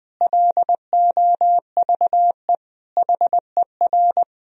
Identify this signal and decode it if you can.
Morse code, heard as a keyed tone, about 21 wpm, 700 Hz. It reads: LOVE HER
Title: Sentences from Top 1000 Words - Review Part 2 - 20wpm